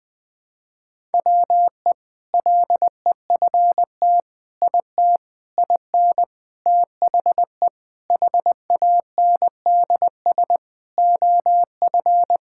Morse code, heard as a keyed tone, about 20 wpm, 700 Hz. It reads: WE LEFT IT IN THE HANDS OF
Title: Sentences with 4-Word Phrases - 20wpm